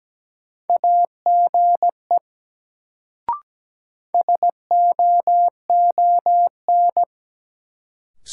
Morse code, heard as a keyed tone, about 17 wpm, 700 Hz. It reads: AGE SOON
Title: Sets of 3 Words - New Word Drill - 17wpm